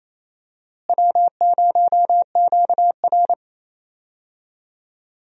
Morse code, heard as a keyed tone, about 28 wpm, 700 Hz. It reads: W0QR
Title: USA Call Signs - 4 Characters - Repeated 2x - 28wpm